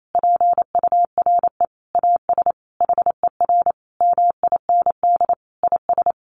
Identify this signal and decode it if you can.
Morse code, heard as a keyed tone, about 28 wpm, 700 Hz. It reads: PURE AS HER MIND IS
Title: Sentences from Top 1500 Words 28wpm